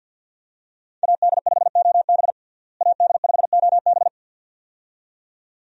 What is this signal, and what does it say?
Morse code, heard as a keyed tone, about 47 wpm, 700 Hz. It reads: AD5OB AD5OB
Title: USA Call Signs - 5 Characters - Repeated 2x - 50wpm